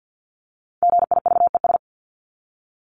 Morse code, heard as a keyed tone, about 50 wpm, 700 Hz. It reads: GIVES
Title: ES:S Suffix - Top 100 Words with Base Words 50wpm